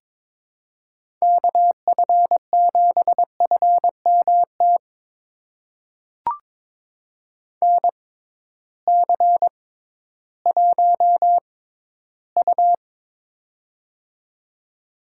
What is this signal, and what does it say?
Morse code, heard as a keyed tone, about 21 wpm, spaced as for 13 wpm, 700 Hz. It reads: KF7FMT NC1U
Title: USA Call Signs - 6x Character Spacing - 22wpm